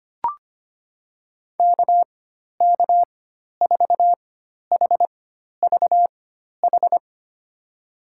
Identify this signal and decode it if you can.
Morse code, heard as a keyed tone, about 25 wpm, spaced as for 12 wpm, 700 Hz. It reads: KK4HVH
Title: USA Call Signs - 4x Character Spacing - 25wpm